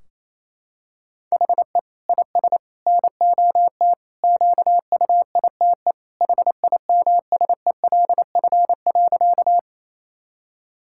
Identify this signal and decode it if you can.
Morse code, heard as a keyed tone, about 28 wpm, 700 Hz. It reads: HE IS NOT QUITE HIMSELF.
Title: Sentences from Top 1600 Words 28wpm